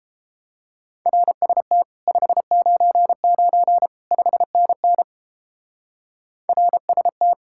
Morse code, heard as a keyed tone, about 33 wpm, 700 Hz. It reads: RST 599 5NN RST